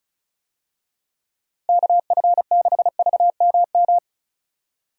Morse code, heard as a keyed tone, about 36 wpm, 700 Hz. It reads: KF6VMM
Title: USA Call Signs - 6 Characters - Repeated 3x - 35wpm